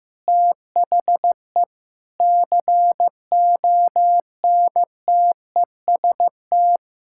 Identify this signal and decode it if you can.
Morse code, heard as a keyed tone, about 15 wpm, 700 Hz. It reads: THE CONTEST
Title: Sentences from Top 1600 Words 15wpm